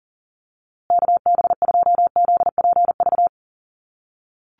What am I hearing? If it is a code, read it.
Morse code, heard as a keyed tone, about 40 wpm, 700 Hz. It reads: KB2ZPV
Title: USA Call Signs Repeated 2x 40wpm